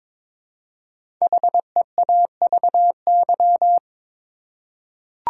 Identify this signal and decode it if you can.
Morse code, heard as a keyed tone, about 23 wpm, 700 Hz. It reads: HEAVY
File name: 5 Letter Words - First 3 Letters Given - 22wpm